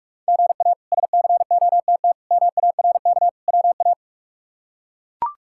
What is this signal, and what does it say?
Morse code, heard as a keyed tone, about 41 wpm, 700 Hz. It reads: GA SCOTT MARK WA
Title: K1USN SST QSO Practice - 45wpm